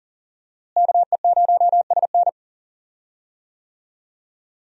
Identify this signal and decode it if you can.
Morse code, heard as a keyed tone, about 40 wpm, 700 Hz. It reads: KE0SN
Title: USA Call Signs - 5 Characters - Repeated 3x - 40wpm